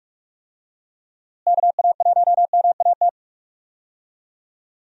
Morse code, heard as a keyed tone, about 46 wpm, 700 Hz. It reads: KA1MAT
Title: USA Call Signs - 6 Characters - Repeated 3x - 45wpm